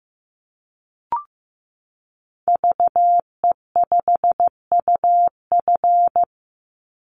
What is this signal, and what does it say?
Morse code, heard as a keyed tone, about 14 wpm, 700 Hz. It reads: VE5UF
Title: Call Signs Repeated 2x - 15wpm